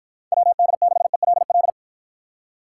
Morse code, heard as a keyed tone, about 53 wpm, 700 Hz. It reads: WD6ELL